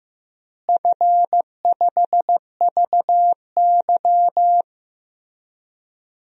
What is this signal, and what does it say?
Morse code, heard as a keyed tone, about 15 wpm, 700 Hz. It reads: F5VY
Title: DXCC-100 Call Signs - 4 Characters - Repeated 3x - 15wpm